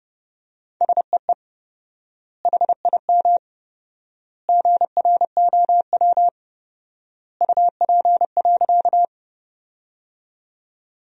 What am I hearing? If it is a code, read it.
Morse code, heard as a keyed tone, about 30 wpm, 700 Hz. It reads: SEE HIM GROW UP.
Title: 4x Word-Spacing - Sentences from Top 300 Words - 30wpm